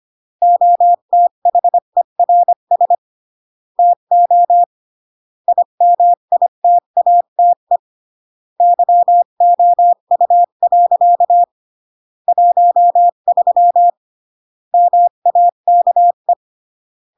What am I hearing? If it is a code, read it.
Morse code, heard as a keyed tone, about 25 wpm, 700 Hz. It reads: OTHERS TO IMITATE YOU. 13 MAKE